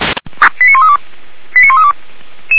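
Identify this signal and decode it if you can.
Five-tone selective call (ZVEI), heard as quick runs of five tones, tones of about 70 ms. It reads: 89132 89132
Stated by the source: Five-tone ZVEI selective call 89110, received by SDR.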